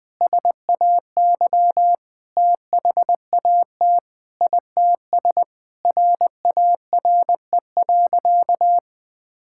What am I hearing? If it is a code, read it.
Morse code, heard as a keyed tone, about 20 wpm, 700 Hz. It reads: SAY THAT ITS RARE.
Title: Sentences with 2-Word Phrases - 20wpm